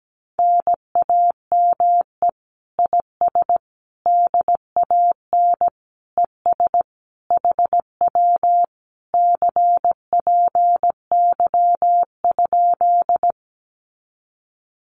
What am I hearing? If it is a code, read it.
Morse code, heard as a keyed tone, about 17 wpm, 700 Hz. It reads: NAME IS DAN ES HW CPY?
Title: Common QSO Phrases Encore 17wpm